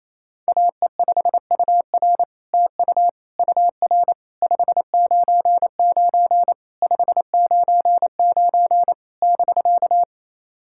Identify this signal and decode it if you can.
Morse code, heard as a keyed tone, about 28 wpm, 700 Hz. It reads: AE5UR TU UR 599 599 <BK>
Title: SOTA QSO Practice 28wpm